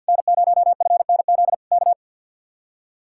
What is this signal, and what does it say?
Morse code, heard as a keyed tone, about 62 wpm, 700 Hz. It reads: N0FNZ K
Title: Speed-Racing Common QSO Phrases - 50wpm